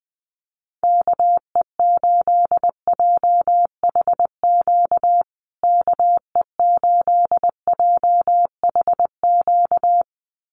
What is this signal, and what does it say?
Morse code, heard as a keyed tone, about 20 wpm, 700 Hz. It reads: KE8JHQ KE8JHQ
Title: USA Call Signs Repeated 2x 20wpm